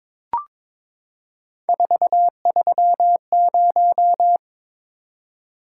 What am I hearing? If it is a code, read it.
Morse code, heard as a keyed tone, about 22 wpm, 700 Hz. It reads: 430
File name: practice / 22wpm / Three Letter-Number Combinations - 22wpm